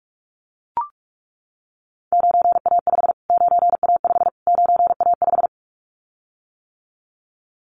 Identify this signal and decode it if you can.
Morse code, heard as a keyed tone, about 44 wpm, 700 Hz. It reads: 9A5 9A5 9A5
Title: Callsign Prefixes for Top 25 DXCC - Repeated 3x - 45wpm